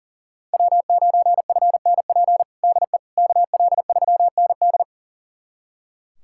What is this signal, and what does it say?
Morse code, heard as a keyed tone, about 40 wpm, 700 Hz. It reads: W9FNP DE KL3ND
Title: USA Callsign DE Callsign - 5 Characters - 40wpm